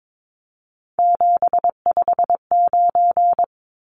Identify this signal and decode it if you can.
Morse code, heard as a keyed tone, about 22 wpm, 700 Hz. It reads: 759
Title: Numbers - 3-Digit 22wpm